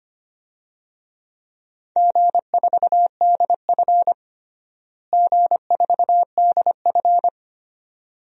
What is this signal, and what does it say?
Morse code, heard as a keyed tone, about 25 wpm, 700 Hz. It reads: G4DF G4DF
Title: DXCC-100 Call Signs - 4 Characters - Repeated 2x - 25wpm